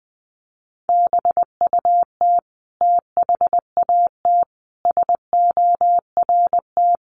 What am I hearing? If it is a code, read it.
Morse code, heard as a keyed tone, about 20 wpm, 700 Hz. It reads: BUT THAT SORT
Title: Sentences from Top 1300 Words 20wpm